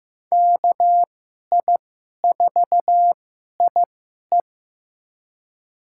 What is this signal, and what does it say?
Morse code, heard as a keyed tone, about 15 wpm, spaced as for 11 wpm, 700 Hz. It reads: KI4IE
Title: USA Call Signs - 5 Characters - 2x Character Spacing - 15wpm